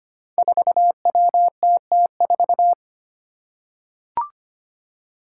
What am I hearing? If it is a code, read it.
Morse code, heard as a keyed tone, about 25 wpm, 700 Hz. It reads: 4WTT4
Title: Five Challenging Letter-Number Combinations 25wpm